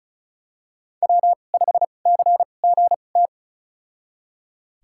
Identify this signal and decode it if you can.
Morse code, heard as a keyed tone, about 36 wpm, spaced as for 25 wpm, 700 Hz. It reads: W5CGT
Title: USA Call Signs - 2x Character Spacing - 35wpm